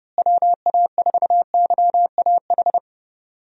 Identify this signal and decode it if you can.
Morse code, heard as a keyed tone, about 30 wpm, 700 Hz. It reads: WA4YAH
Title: USA Call Signs - 6 Characters - Repeated 2x - 30wpm